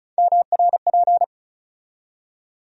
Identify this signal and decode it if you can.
Morse code, heard as a keyed tone, about 36 wpm, 700 Hz. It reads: MRP
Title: USA Call Sign Postfixes - 3 Letters 35wpm